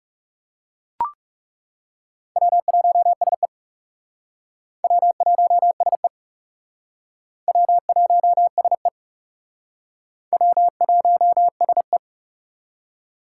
Morse code, heard as a keyed tone, about 36 wpm, 700 Hz. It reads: W1SE W1SE W1SE W1SE
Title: Speed Racing - USA 1x2 Call Signs 30wpm